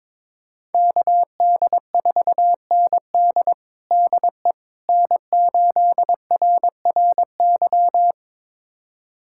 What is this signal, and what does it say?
Morse code, heard as a keyed tone, about 22 wpm, 700 Hz. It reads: KD4ND DE N8RRY
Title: USA Callsign DE Callsign - 5 Characters - 22wpm